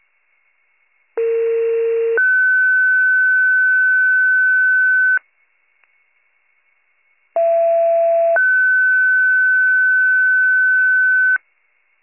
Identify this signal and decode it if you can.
Two-tone page, heard as a one-second tone, then a three-second tone, about 460 then 1500 Hz, about 670 then 1500 Hz. It is heard from 1.2 to 11.4 s.